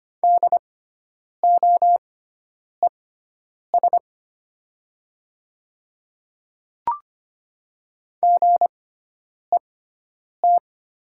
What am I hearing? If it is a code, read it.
Morse code, heard as a keyed tone, about 25 wpm, spaced as for 9 wpm, 700 Hz. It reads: DOES GET
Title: 6x Character Spacing - Top 200 Words - 25wpm